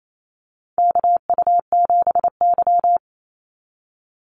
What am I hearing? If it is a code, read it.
Morse code, heard as a keyed tone, about 28 wpm, 700 Hz. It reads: KU7Y
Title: Call Signs Repeated 2x - 28wpm